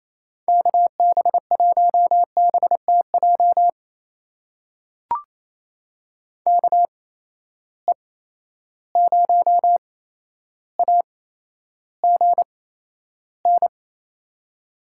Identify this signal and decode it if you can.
Morse code, heard as a keyed tone, about 28 wpm, spaced as for 12 wpm, 700 Hz. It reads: KB1BTJ KE0AGN
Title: USA Call Signs - 6 Characters - 8x Character Spacing - 28wpm